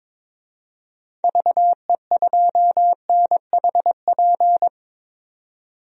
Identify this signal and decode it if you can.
Morse code, heard as a keyed tone, about 22 wpm, 700 Hz. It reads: VE2NHP
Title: DXCC-100 Call Signs - 6 Characters - 22wpm